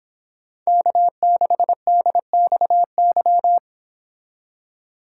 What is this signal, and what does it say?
Morse code, heard as a keyed tone, about 26 wpm, 700 Hz. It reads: K6DXY
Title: Speed Racing - USA 1x3 Call Signs 22wpm